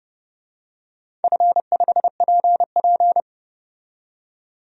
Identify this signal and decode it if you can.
Morse code, heard as a keyed tone, about 28 wpm, 700 Hz. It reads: F5PP
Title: DXCC-100 Call Signs - 4 Characters - 30wpm